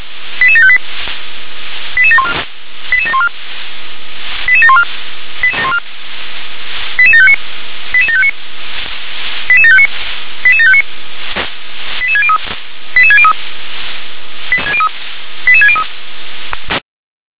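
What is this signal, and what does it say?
Five-tone selective call (ZVEI), heard as quick runs of five tones, tones of about 70 ms. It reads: 88758 88714 88714 88714 88759 88759 88759 88759 88603 88603 88603
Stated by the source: ZVEI five-tone selective call received on SDR.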